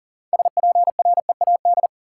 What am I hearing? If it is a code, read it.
Morse code, heard as a keyed tone, about 40 wpm, 700 Hz. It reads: SPREAD
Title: Sentences from Top 1200 Words - New Word Drill - 40wpm